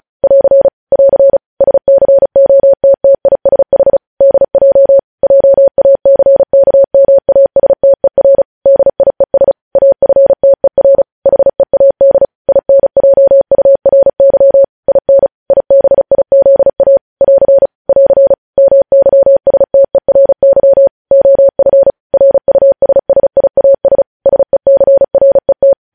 Morse code, heard as a keyed tone, about 36 wpm, 550 Hz. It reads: <AR> <AR> SCOTTISH DJ JACKMASTER DIES AFTER HEAD INJURY IN IBIZA <AR> <AR> MYSTERY OF RUSSIAS SECRET